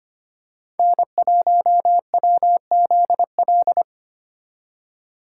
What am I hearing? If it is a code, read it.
Morse code, heard as a keyed tone, about 29 wpm, 700 Hz. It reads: N1WZL
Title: Speed Racing - USA 1x3 Call Signs 25wpm